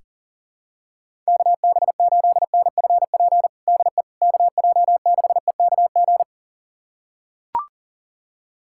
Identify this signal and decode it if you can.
Morse code, heard as a keyed tone, about 40 wpm, 700 Hz. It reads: KB8NFP DE KJ6EKG
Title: USA Callsign DE Callsign - 6 Characters - 40wpm